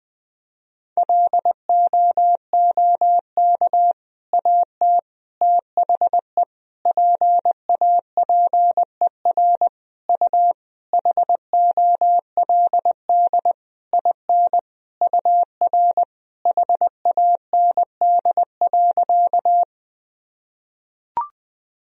Morse code, heard as a keyed tone, about 20 wpm, 700 Hz. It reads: LOOK AT THE PAPER U HOLD IN UR HAND.
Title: Sentences from Top 400 Words and Abbreviations 20wpm